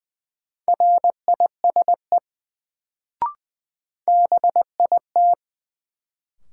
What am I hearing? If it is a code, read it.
Morse code, heard as a keyed tone, about 20 wpm, 700 Hz. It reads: RISE BIT
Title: Sentences from Top 900 Words - New Word Drill - 20wpm